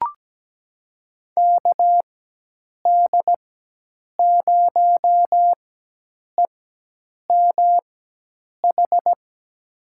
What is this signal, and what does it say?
Morse code, heard as a keyed tone, about 17 wpm, spaced as for 8 wpm, 700 Hz. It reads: KD0EMH
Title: USA Call Signs - 6 Characters - 4x Character Spacing - 17wpm